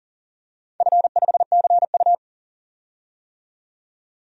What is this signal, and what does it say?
Morse code, heard as a keyed tone, about 40 wpm, 700 Hz. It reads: F5CU
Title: DXCC-100 Call Signs - 4 Characters - 40wpm